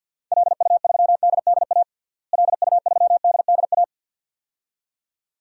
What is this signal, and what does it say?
Morse code, heard as a keyed tone, about 53 wpm, 700 Hz. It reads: LU3DDA LU3DDA
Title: DXCC-100 Call Signs - 6 Characters - Repeated 2x - 50wpm